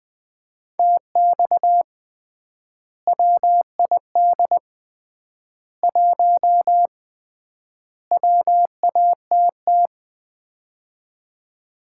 Morse code, heard as a keyed tone, about 20 wpm, 700 Hz. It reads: TX WID 1 WATT
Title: Common QSO Phrases - 3x Word-Spacing - 20wpm